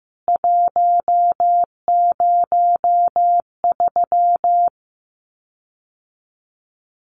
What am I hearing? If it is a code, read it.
Morse code, heard as a keyed tone, about 15 wpm, 700 Hz. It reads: 103
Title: CWT Exchange 15wpm